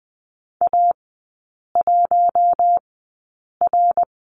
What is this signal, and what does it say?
Morse code, heard as a keyed tone, about 20 wpm, spaced as for 8 wpm, 700 Hz. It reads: A1R
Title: 060 Lesson 11e - Letter h - Three Characters 20wpm